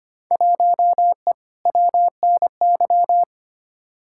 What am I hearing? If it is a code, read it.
Morse code, heard as a keyed tone, about 25 wpm, 700 Hz. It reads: 1E WNY
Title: Field Day Exchanges - 25wpm